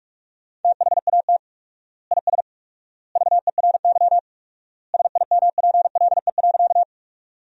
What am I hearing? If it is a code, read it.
Morse code, heard as a keyed tone, about 46 wpm, 700 Hz. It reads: THAT IS VERY SIMPLE.
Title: 4x Word-Spacing - Sentences from Top 400 Words - 45wpm